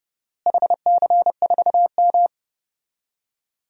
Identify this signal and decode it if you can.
Morse code, heard as a keyed tone, about 30 wpm, 700 Hz. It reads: HC4M
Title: DXCC-100 Call Signs - 4 Characters - Repeated 3x - 30wpm